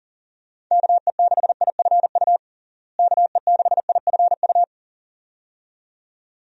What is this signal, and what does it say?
Morse code, heard as a keyed tone, about 40 wpm, 700 Hz. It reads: KE6IFU KE6IFU
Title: USA Call Signs - 6 Characters - Repeated 2x - 40wpm